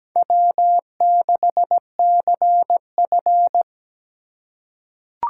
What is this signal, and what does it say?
Morse code, heard as a keyed tone, about 17 wpm, 700 Hz. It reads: W6CF
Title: USA Call Signs - 4 Characters - 3x Character Spacing - 17wpm